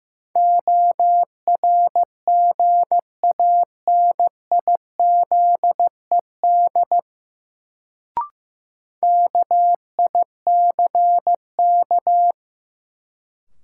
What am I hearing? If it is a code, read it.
Morse code, heard as a keyed tone, about 16 wpm, 700 Hz. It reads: ORGANIZED KICK
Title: Sentences from Top 1300 Words - New Word Drill Alt - 15wpm